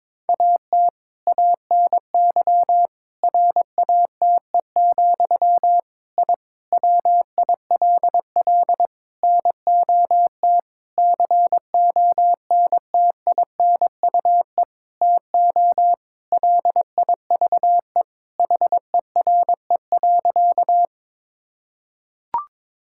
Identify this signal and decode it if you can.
Morse code, heard as a keyed tone, about 22 wpm, 700 Hz. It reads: AT ANY RATE, I WILL NOT CONTINUE TO LIVE HERE.
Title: Sentences from Top 1000 Words - Review Part 6 - 22wpm